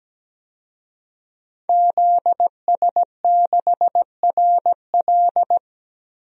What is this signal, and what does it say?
Morse code, heard as a keyed tone, about 17 wpm, 700 Hz. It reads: ZS6RL
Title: DXCC-100 Call Signs - 5 Characters - Repeated 3x - 17wpm